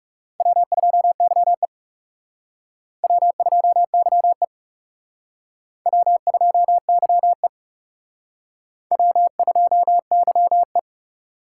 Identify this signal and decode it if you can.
Morse code, heard as a keyed tone, about 37 wpm, 700 Hz. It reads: W2YE W2YE W2YE W2YE
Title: Speed Racing - USA 1x2 Call Signs 30wpm